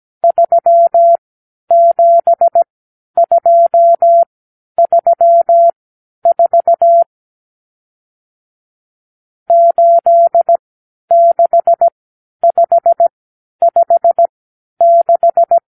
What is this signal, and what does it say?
Morse code, heard as a keyed tone, about 17 wpm, spaced as for 10 wpm, 680 Hz. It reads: 37234 86556
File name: k7qo / Lesson 54 (9)